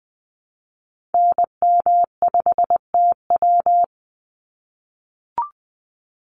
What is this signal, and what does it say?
Morse code, heard as a keyed tone, about 20 wpm, 700 Hz. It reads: NM5TW
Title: NA SOTA Activator Callsigns 20wpm